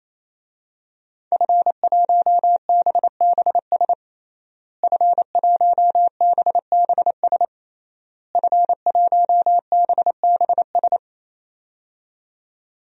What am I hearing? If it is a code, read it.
Morse code, heard as a keyed tone, about 28 wpm, 700 Hz. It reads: F1BBS F1BBS F1BBS
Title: DXCC-100 Call Signs - 5 Characters - Repeated 3x - 28wpm